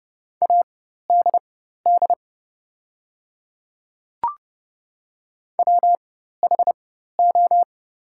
Morse code, heard as a keyed tone, about 30 wpm, spaced as for 14 wpm, 700 Hz. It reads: ADD WHO
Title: 4x Character Spacing - Top 300 Words - 30wpm